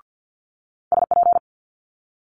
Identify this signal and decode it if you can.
Morse code, heard as a keyed tone, about 50 wpm, 700 Hz. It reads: SP
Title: Multi-letter Phonemes Quick Fire 50wpm